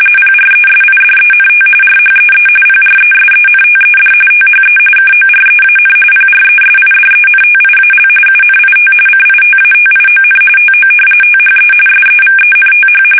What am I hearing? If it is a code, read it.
RTTY, about 45.45 baud. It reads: PRESIDENT RHEE PRESENTED THE KOREAN PRESIDENTIAL UNIT CITATION TO THE SEVENTH
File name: fsk850 / battleshipnj